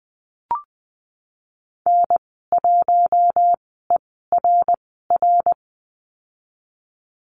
Morse code, heard as a keyed tone, about 20 wpm, spaced as for 15 wpm, 700 Hz. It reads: N1ERR
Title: 147 Lesson 24g - Letter y - Call Signs 20wpm